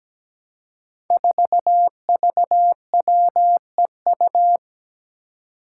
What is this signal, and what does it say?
Morse code, heard as a keyed tone, about 17 wpm, 700 Hz. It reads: 4VWEU
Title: Five Challenging Letter-Number Combinations 17wpm